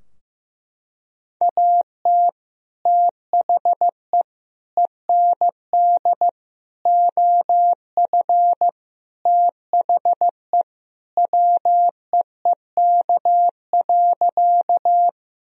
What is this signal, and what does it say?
Morse code, heard as a keyed tone, about 15 wpm, 700 Hz. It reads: AT THE END OF THE WEEK.
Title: Sentences from Top 1000 Words - Review Part 2 - 15wpm